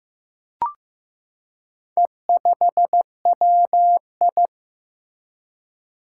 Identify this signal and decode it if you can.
Morse code, heard as a keyed tone, about 15 wpm, 700 Hz. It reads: E5WI
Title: Four Challenging Letter-Number Combinations 15wpm